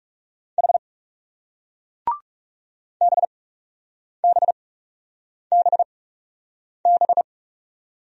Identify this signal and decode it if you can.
Morse code, heard as a keyed tone, about 40 wpm, spaced as for 8 wpm, 700 Hz. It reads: H BBBB